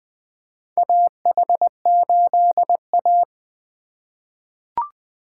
Morse code, heard as a keyed tone, about 20 wpm, 700 Hz. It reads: AH8A
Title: USA Call Signs - 4 Characters - 3x Character Spacing - 20wpm